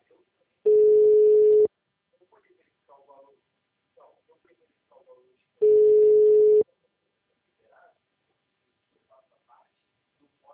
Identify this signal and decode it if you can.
Ringback tone (Europe, 425 Hz), with speech.
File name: out- -2004- - - .98566